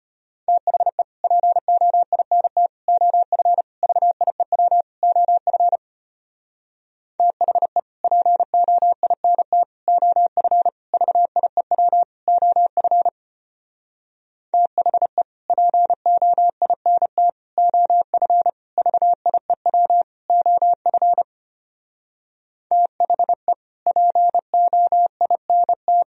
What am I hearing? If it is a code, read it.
Morse code, heard as a keyed tone, about 33 wpm, 700 Hz. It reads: THE POINT OF VIEW OF THE POINT OF VIEW OF THE POINT OF VIEW OF THE POINT